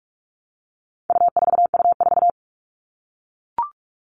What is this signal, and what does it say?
Morse code, heard as a keyed tone, about 44 wpm, 700 Hz. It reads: U4U4
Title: All Cut Numbers - 4-Digit 45wpm